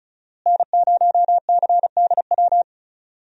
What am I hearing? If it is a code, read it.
Morse code, heard as a keyed tone, about 36 wpm, 700 Hz. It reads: N0CDW
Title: USA Call Signs - 5 Characters - Repeated 2x - 35wpm